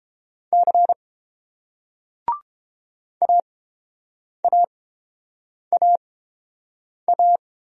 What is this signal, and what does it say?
Morse code, heard as a keyed tone, about 30 wpm, spaced as for 7 wpm, 700 Hz. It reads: C AAAA